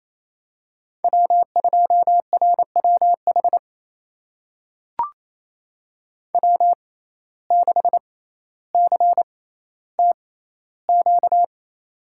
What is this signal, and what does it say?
Morse code, heard as a keyed tone, about 27 wpm, spaced as for 14 wpm, 700 Hz. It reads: W2RWH W6CTQ
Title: USA Call Signs - 5 Characters - 6x Character Spacing - 28wpm